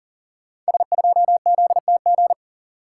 Morse code, heard as a keyed tone, about 38 wpm, 700 Hz. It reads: S2ZTG
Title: Code Groups - 5 Letter:Numbers - 40wpm